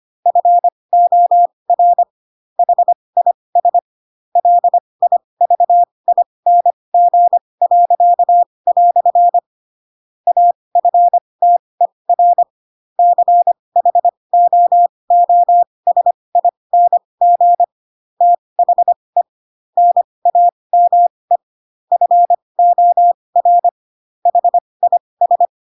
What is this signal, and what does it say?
Morse code, heard as a keyed tone, about 25 wpm, spaced as for 20 wpm, 700 Hz. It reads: FOR HIS LIVING." AFTER CHOOSING THE NAME FOR HIS